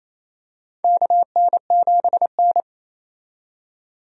Morse code, heard as a keyed tone, about 28 wpm, 700 Hz. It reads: KN7N
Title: USA Call Signs - 4 Characters - 28wpm